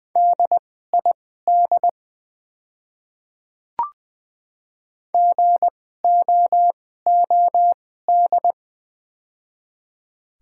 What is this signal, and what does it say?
Morse code, heard as a keyed tone, about 20 wpm, spaced as for 15 wpm, 700 Hz. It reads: DID GOOD
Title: 2x Character Spacing - Top 300 Words - 20wpm